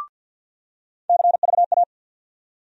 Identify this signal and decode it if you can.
Morse code, heard as a keyed tone, about 51 wpm, 700 Hz. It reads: CVA
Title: USA Call Sign Postfixes - 3 Letters 50wpm